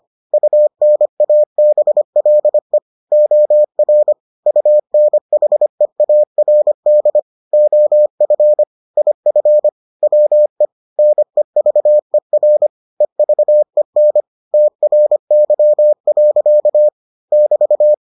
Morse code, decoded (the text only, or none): UNABLE OR UNHEARD OF IF WE NEVER EVEN TRY. <BT>